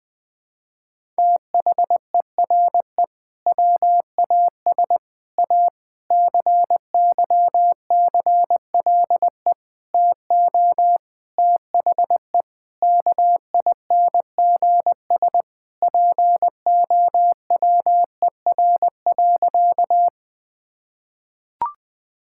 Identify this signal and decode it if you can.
Morse code, heard as a keyed tone, about 20 wpm, 700 Hz. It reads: THERE WAS A CYCLE TO THE KINGS POWER.
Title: Sentences from Top 1000 Words - Review Part 6 - 20wpm